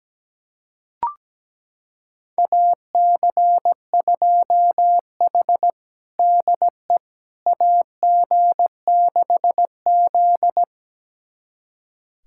Morse code, heard as a keyed tone, about 17 wpm, 700 Hz. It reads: AC2H DE AG6Z
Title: USA Callsign DE Callsign - 4 Characters - 17wpm